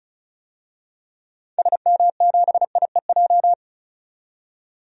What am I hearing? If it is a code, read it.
Morse code, heard as a keyed tone, about 35 wpm, 700 Hz. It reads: SM7IEJ
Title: DXCC-100 Call Signs - 6 Characters - 35wpm